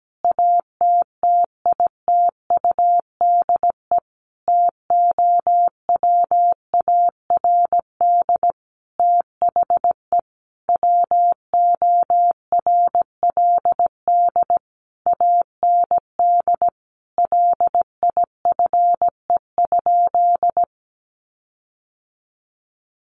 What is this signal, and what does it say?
Morse code, heard as a keyed tone, about 17 wpm, 700 Hz. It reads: ATTITUDE TOWARD THE WORLD AND LIFE?
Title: Sentences from Top 1300 Words 17wpm